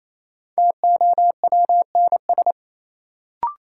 Morse code, heard as a keyed tone, about 28 wpm, 700 Hz. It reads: TOWNS
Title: Sentences from Top 1000 Words - Review Part 6 - New Word Drill - 28wpm